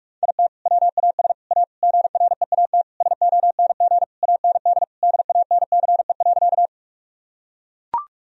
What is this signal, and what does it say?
Morse code, heard as a keyed tone, about 44 wpm, 700 Hz. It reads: IT WAS A GREAT SONG AND DANCE.
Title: Sentences from Top 1000 Words - Review Part 4 - 45wpm